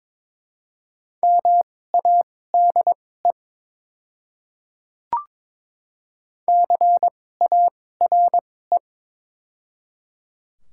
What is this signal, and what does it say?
Morse code, heard as a keyed tone, about 22 wpm, spaced as for 16 wpm, 700 Hz. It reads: MADE CARE
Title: 2x Character Spacing - Top 300 Words - 22wpm